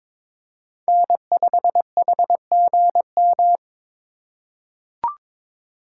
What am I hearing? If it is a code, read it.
Morse code, heard as a keyed tone, about 22 wpm, 700 Hz. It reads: N5HGM
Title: USA Call Signs - 5 Characters - 2x Character Spacing - 22wpm